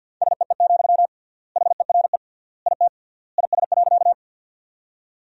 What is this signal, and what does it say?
Morse code, heard as a keyed tone, about 52 wpm, 700 Hz. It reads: SEE, HERE IT IS.
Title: 3x Word-Spacing - Sentences from Top 200 Words - 50wpm